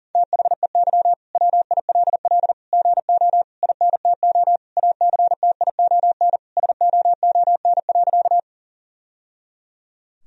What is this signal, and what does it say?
Morse code, heard as a keyed tone, about 40 wpm, 700 Hz. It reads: THEY WILL GO INTO ACTION SOON.
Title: Sentences from Top 1000 Words - Review Part 4 - 40wpm